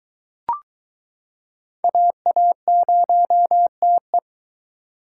Morse code, heard as a keyed tone, about 22 wpm, 700 Hz. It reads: AA0TE